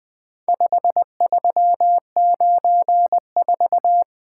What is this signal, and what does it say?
Morse code, heard as a keyed tone, about 20 wpm, 700 Hz. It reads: 5394
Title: Four Letter-Number Combinations - 20wpm